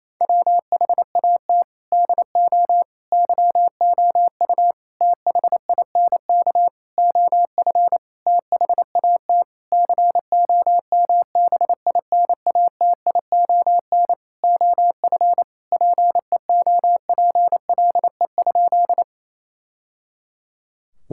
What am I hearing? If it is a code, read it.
Morse code, heard as a keyed tone, about 28 wpm, 700 Hz. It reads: WHAT DO YOU THINK OF THAT COMBINATION OF PEOPLE?
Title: Sentences from Top 1000 Words - Review Part 6 - 28wpm